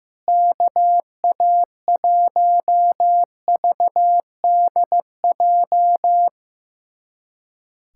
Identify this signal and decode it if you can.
Morse code, heard as a keyed tone, about 15 wpm, 700 Hz. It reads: KA1VDJ
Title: USA Callsign DE Callsign - 6 Characters - 15wpm